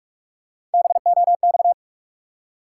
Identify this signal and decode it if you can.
Morse code, heard as a keyed tone, about 46 wpm, 700 Hz. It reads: BOX
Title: Three Letter Words - 45wpm